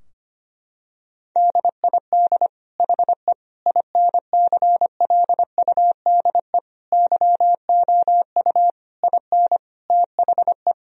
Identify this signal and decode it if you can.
Morse code, heard as a keyed tone, about 25 wpm, 700 Hz. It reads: DID HE INCLUDE YOU IN THE
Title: Sentences from Top 1000 Words - Review Part 4 - 25wpm